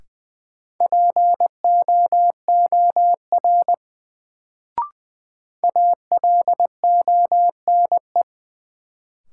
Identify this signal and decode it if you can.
Morse code, heard as a keyed tone, about 20 wpm, 700 Hz. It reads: POOR ALONE
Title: Sentences from Top 1000 Words - New Word Drill - 20wpm